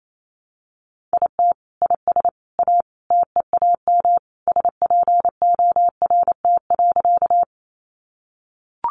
Morse code, heard as a keyed tone, about 28 wpm, 700 Hz. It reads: IT IS A TEAM SPORT.
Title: Sentences from Top 1500 Words 28wpm